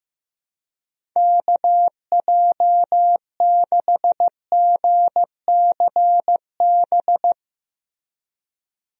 Morse code, heard as a keyed tone, about 15 wpm, 700 Hz. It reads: KJ6GCB
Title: USA Call Signs - 6 Characters - Repeated 2x - 15wpm